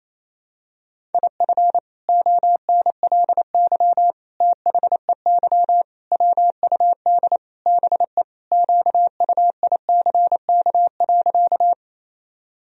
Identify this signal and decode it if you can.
Morse code, heard as a keyed tone, about 28 wpm, 700 Hz. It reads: IF ONLY THEY WUD BE QUICK.
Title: Sentences from Top 500 Words and Abbreviations 28wpm